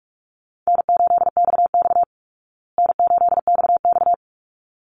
Morse code, heard as a keyed tone, about 44 wpm, 700 Hz. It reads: N8XX N8XX
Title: Easy Call Signs Repeated 2x - 45wpm